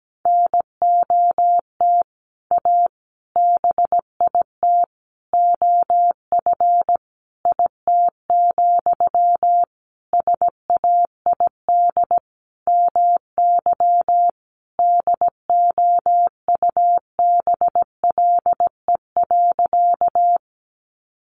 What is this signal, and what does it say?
Morse code, heard as a keyed tone, about 17 wpm, 700 Hz. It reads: NOT A BIT OF IT, SAID MY DOUBLE.
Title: Sentences from Top 1400 Words 17wpm